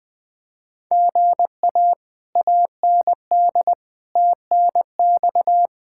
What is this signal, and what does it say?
Morse code, heard as a keyed tone, about 20 wpm, 700 Hz. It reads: GA AND TNX
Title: Speed-Racing Common QSO Phrases - 17wpm